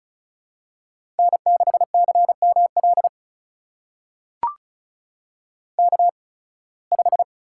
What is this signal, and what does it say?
Morse code, heard as a keyed tone, about 36 wpm, spaced as for 23 wpm, 700 Hz. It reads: N6CML K5
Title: USA Call Signs - 8x Character Spacing - 35wpm